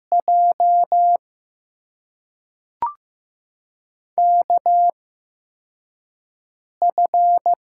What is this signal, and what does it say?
Morse code, heard as a keyed tone, about 16 wpm, spaced as for 3 wpm, 700 Hz. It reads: JKF
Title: USA Call Signs - 6 Characters - 8x Character Spacing - 15wpm